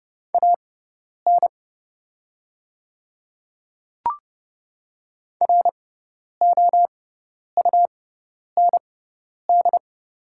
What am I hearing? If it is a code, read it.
Morse code, heard as a keyed tone, about 30 wpm, spaced as for 10 wpm, 700 Hz. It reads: AN ROUND
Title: 6x Character Spacing - Top 300 Words - 30wpm